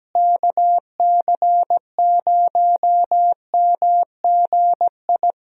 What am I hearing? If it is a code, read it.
Morse code, heard as a keyed tone, about 17 wpm, 700 Hz. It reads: KC0MGI
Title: USA Call Signs - 6 Characters - Repeated 3x - 17wpm